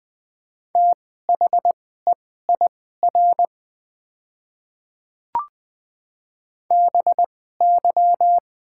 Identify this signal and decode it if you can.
Morse code, heard as a keyed tone, about 20 wpm, spaced as for 15 wpm, 700 Hz. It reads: THEIR BY
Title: 2x Character Spacing - Top 100 Words - 20wpm